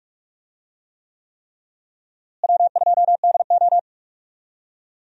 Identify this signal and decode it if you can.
Morse code, heard as a keyed tone, about 41 wpm, 700 Hz. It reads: W2DO
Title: USA Call Signs - 4 Characters - Repeated 3x - 45wpm